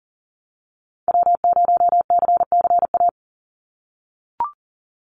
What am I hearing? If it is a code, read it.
Morse code, heard as a keyed tone, about 40 wpm, 700 Hz. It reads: W0CCA
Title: NA SOTA Activator Callsigns 40wpm